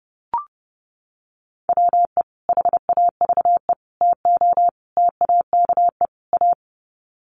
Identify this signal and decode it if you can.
Morse code, heard as a keyed tone, about 30 wpm, 700 Hz. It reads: WE HAVE TO TAKE A
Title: Sets of 5 Words Encore 30wpm